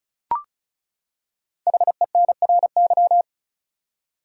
Morse code, heard as a keyed tone, about 36 wpm, 700 Hz. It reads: HENRY
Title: USA Common Names - Max 5 Letters - 35wpm